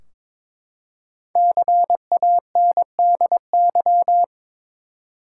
Morse code, heard as a keyed tone, about 22 wpm, 700 Hz. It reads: CANDY
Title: Five Letter Words - 22wpm